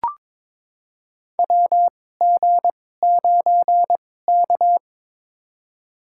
Morse code, heard as a keyed tone, about 22 wpm, spaced as for 16 wpm, 700 Hz. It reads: WG9K
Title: USA Call Signs - 2x Character Spacing - 22wpm